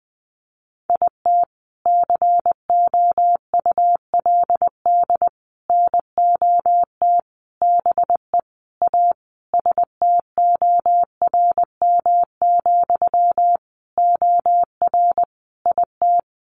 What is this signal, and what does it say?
Morse code, heard as a keyed tone, about 20 wpm, 700 Hz. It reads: IT COULD NOT BE A STORM, OR IT
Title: Sentences from Top 1500 Words 20wpm